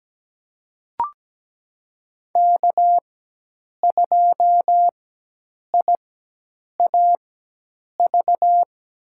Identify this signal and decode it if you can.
Morse code, heard as a keyed tone, about 17 wpm, spaced as for 8 wpm, 700 Hz. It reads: K2IAV
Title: USA Call Signs - 5 Characters - 4x Character Spacing - 17wpm